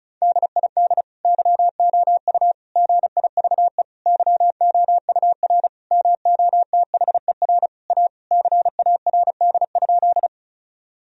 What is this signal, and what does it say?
Morse code, heard as a keyed tone, about 36 wpm, 700 Hz. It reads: DID YOU GIVE YOUR MOTHER A CARD?
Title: Sentences from Top 1000 Words - Review Part 4 - 35wpm